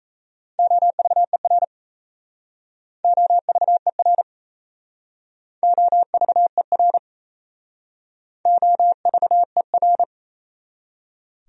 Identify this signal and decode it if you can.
Morse code, heard as a keyed tone, about 35 wpm, 700 Hz. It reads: OVER OVER OVER OVER